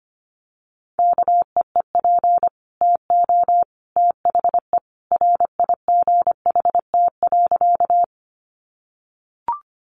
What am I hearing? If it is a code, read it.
Morse code, heard as a keyed tone, about 25 wpm, 700 Hz. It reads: KEEP TO THE RIGHT.
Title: Sentences from Top 300 Words 25wpm